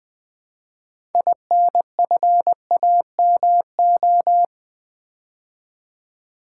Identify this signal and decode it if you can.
Morse code, heard as a keyed tone, about 21 wpm, 700 Hz. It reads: INFAMO
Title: Build A Word - 2 Letters At A Time - 8-Letter Words - 20wpm